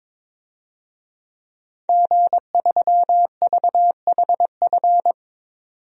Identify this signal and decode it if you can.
Morse code, heard as a keyed tone, about 22 wpm, 700 Hz. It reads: G3VHF
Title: DXCC-100 Call Signs - 5 Characters - 22wpm